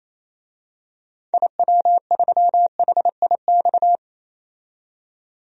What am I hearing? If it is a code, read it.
Morse code, heard as a keyed tone, about 28 wpm, 700 Hz. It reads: IW3HIX
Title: DXCC-100 Call Signs - 6 Characters - 28wpm